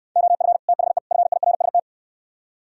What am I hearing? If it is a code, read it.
Morse code, heard as a keyed tone, about 67 wpm, 700 Hz. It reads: OF THE FIRST